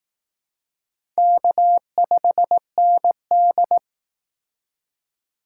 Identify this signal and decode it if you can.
Morse code, heard as a keyed tone, about 18 wpm, 700 Hz. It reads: K5ND
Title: Speed Racing - USA Call Signs 15wpm